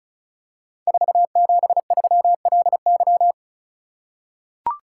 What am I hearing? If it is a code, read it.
Morse code, heard as a keyed tone, about 36 wpm, 700 Hz. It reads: 473LY
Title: Code Groups - 5 Letter:Numbers - 35wpm